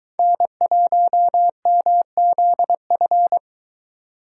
Morse code, heard as a keyed tone, about 23 wpm, 700 Hz. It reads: N1MZF